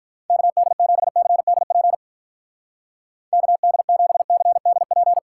Morse code, heard as a keyed tone, about 45 wpm, 700 Hz. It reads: KD7CDP KD7CDP